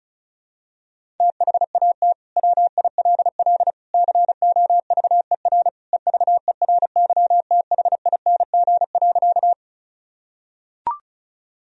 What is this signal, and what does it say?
Morse code, heard as a keyed tone, about 36 wpm, 700 Hz. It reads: THAT WILL COVER EVERYTHING.
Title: Sentences from Top 1000 Words - Review Part 1 - 35wpm